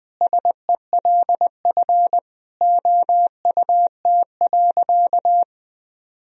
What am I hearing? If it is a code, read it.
Morse code, heard as a keyed tone, about 20 wpm, 700 Hz. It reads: SELF OUT.
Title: Sentences from Top 1000 Words - Review Part 1 - 20wpm